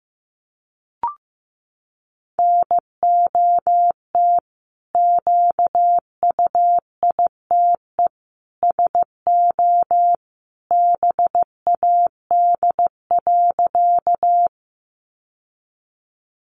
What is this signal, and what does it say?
Morse code, heard as a keyed tone, about 15 wpm, 700 Hz. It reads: NOT QUITE SO BAD.
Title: Sentences from Top 700 Words 15wpm